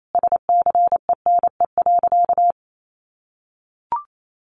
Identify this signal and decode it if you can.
Morse code, heard as a keyed tone, about 28 wpm, 700 Hz. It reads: SCENE.
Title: Sentences from Top 1300 Words 28wpm